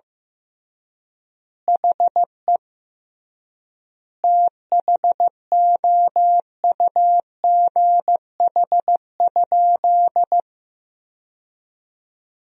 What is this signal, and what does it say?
Morse code, heard as a keyed tone, about 15 wpm, 700 Hz. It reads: HE THOUGH?
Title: 3x Word-Spacing - Sentences from Top 400 Words - 15wpm